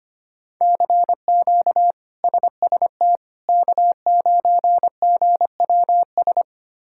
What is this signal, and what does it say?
Morse code, heard as a keyed tone, about 25 wpm, 700 Hz. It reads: CQ SST K9GWS
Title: K1USN SST QSO Practice - 25wpm